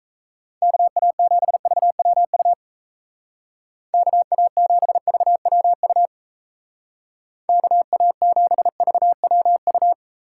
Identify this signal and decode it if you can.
Morse code, heard as a keyed tone, about 34 wpm, 700 Hz. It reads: KA7VWU KA7VWU KA7VWU
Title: Speed Racing - USA 2x3 Call Signs 28wpm